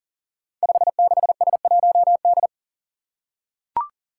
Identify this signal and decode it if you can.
Morse code, heard as a keyed tone, about 40 wpm, 700 Hz. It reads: 56S1D
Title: Five Challenging Letter-Number Combinations 40wpm